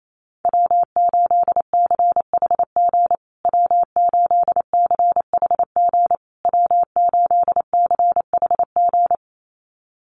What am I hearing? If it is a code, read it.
Morse code, heard as a keyed tone, about 28 wpm, 700 Hz. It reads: W8CHG W8CHG W8CHG
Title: USA Call Signs Repeated 3x 28wpm